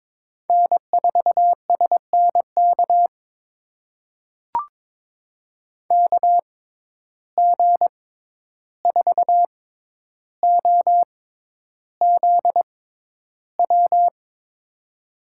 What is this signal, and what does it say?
Morse code, heard as a keyed tone, about 22 wpm, spaced as for 11 wpm, 700 Hz. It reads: N4SNK KG4OZW
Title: USA Call Signs - 6x Character Spacing - 22wpm